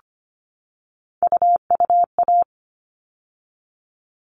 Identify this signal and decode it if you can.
Morse code, heard as a keyed tone, about 25 wpm, 700 Hz. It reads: UUA
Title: All Cut Numbers - 3-Digit 25wpm